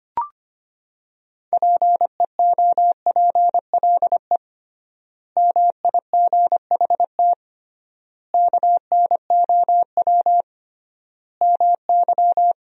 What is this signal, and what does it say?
Morse code, heard as a keyed tone, about 25 wpm, 700 Hz. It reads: PEOPLE MIGHT KNOW MY
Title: 3x Word-Spacing - Sentences from Top 500 Words - 25wpm